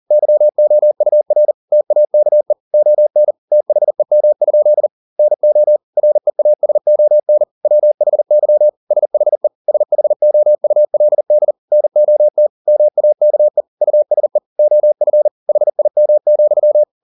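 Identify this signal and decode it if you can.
Morse code, heard as a keyed tone, about 40 wpm, 600 Hz. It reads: YOUR TAKE ON THEM? NO REASON WHY SHE SHOULD NOT MAKE USE OF HIM,
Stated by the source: Morse code at 40 wpm, practice text, CW med Gnister.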